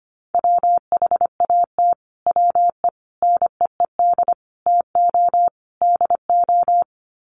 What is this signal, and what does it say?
Morse code, heard as a keyed tone, about 25 wpm, 700 Hz. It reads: WHAT WE NEED TO DO
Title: Sets of 5 Words Encore 25wpm